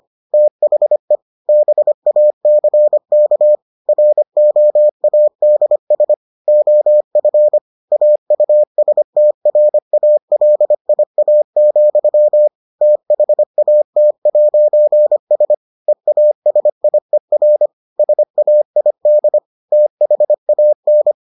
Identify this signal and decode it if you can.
Morse code, heard as a keyed tone, about 25 wpm, 600 Hz. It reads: THE BACK ROADS OF AUSTRALIA, THAT'S EASIER SAID THAN